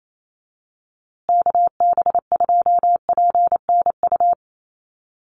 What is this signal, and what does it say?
Morse code, heard as a keyed tone, about 28 wpm, 700 Hz. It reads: KB2PNU
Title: USA Call Signs Repeated 2x 28wpm